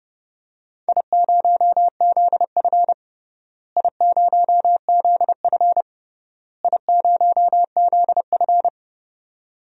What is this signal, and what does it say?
Morse code, heard as a keyed tone, about 30 wpm, 700 Hz. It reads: I0ZF I0ZF I0ZF
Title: DXCC-100 Call Signs - 4 Characters - Repeated 3x - 30wpm